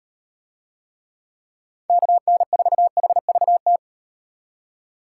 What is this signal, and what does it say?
Morse code, heard as a keyed tone, about 38 wpm, 700 Hz. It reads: KN4HVT